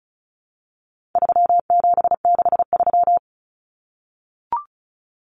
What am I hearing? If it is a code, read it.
Morse code, heard as a keyed tone, about 34 wpm, 700 Hz. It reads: 3763
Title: Common Cut Numbers - 4-Digit 35wpm